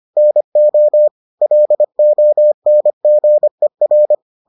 Morse code, heard as a keyed tone, about 25 wpm, 600 Hz. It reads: NO LONGER